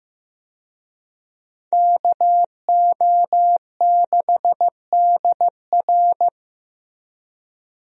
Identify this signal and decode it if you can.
Morse code, heard as a keyed tone, about 15 wpm, 700 Hz. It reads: KO6DR
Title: USA Call Signs - 5 Characters - Repeated 2x - 15wpm